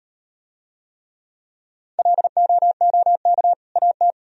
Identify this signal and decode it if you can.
Morse code, heard as a keyed tone, about 36 wpm, 700 Hz. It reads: LOOK AT